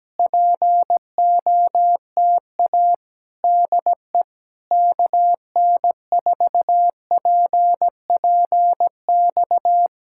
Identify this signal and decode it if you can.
Morse code, heard as a keyed tone, about 17 wpm, 700 Hz. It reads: POTA DE KN4PPX
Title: Speed-Racing Common QSO Phrases Encore - 17wpm